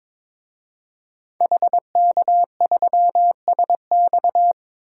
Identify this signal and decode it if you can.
Morse code, heard as a keyed tone, about 22 wpm, 700 Hz. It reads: HK3SX
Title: DXCC-100 Call Signs - 5 Characters - 22wpm